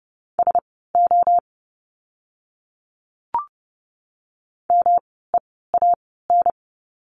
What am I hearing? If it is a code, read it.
Morse code, heard as a keyed tone, about 30 wpm, spaced as for 17 wpm, 700 Hz. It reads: SO MEAN
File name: course / 30:10wpm / 230 Lesson 37f - Letter x - Words 30wpm